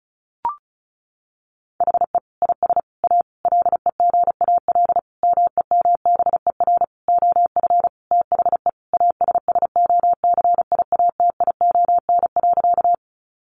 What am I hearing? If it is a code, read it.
Morse code, heard as a keyed tone, about 34 wpm, 700 Hz. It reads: HE IS A LEGAL MEMBER OF THE ASSOCIATION.
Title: Sentences from Top 1200 Words 35wpm